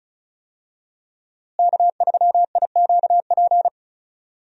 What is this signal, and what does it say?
Morse code, heard as a keyed tone, about 34 wpm, 700 Hz. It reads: K3IQP
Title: USA Call Signs - 5 Characters - Repeated 3x - 35wpm